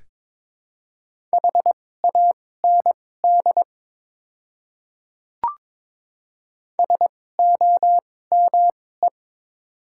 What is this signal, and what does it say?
Morse code, heard as a keyed tone, about 22 wpm, spaced as for 16 wpm, 700 Hz. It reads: HAND SOME
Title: 2x Character Spacing - Top 500 Words - 22wpm